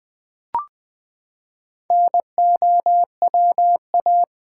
Morse code, heard as a keyed tone, about 20 wpm, 700 Hz. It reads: NOWA
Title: Build A Word - 4 Letters At A Time - 8-Letter Words - 20wpm